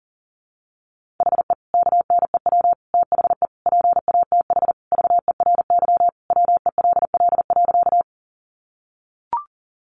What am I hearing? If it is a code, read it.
Morse code, heard as a keyed tone, about 40 wpm, 700 Hz. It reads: HE KNEW THE PATH VERY WELL.
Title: Sentences from Top 1400 Words 40wpm